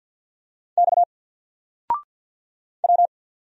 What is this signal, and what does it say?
Morse code, heard as a keyed tone, about 48 wpm, 700 Hz. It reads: X W
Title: Single Character (Letter:Number:Punctuation:Prosign) 50wpm